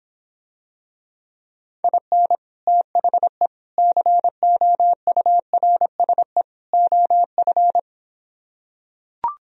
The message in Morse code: IN THE COURSE OF

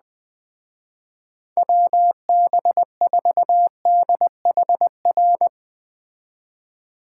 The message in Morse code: WB4DHR

